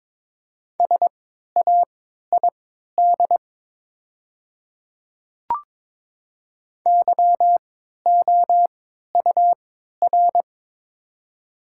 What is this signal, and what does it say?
Morse code, heard as a keyed tone, about 23 wpm, spaced as for 13 wpm, 700 Hz. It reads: SAID YOUR
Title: 3x Character Spacing - Top 500 Words - 22wpm